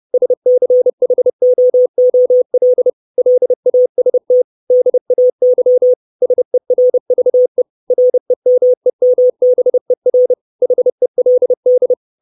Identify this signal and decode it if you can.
Morse code, heard as a keyed tone, about 30 wpm, 500 Hz. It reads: SCHOOL LAST DAY SERVE REMEMBER HELD
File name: 632 Words 30WPM